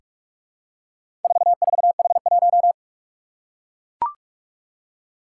Morse code, heard as a keyed tone, about 41 wpm, 700 Hz. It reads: 44H1
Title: Four Challenging Letter-Number Combinations 45wpm